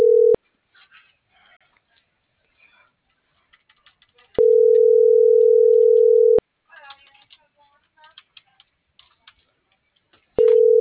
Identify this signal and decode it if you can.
Ringback tone (North America), 440 + 480 Hz.